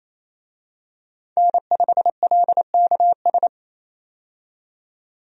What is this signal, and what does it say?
Morse code, heard as a keyed tone, about 28 wpm, 700 Hz. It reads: N5LKS
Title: USA Call Signs - 5 Characters - 28wpm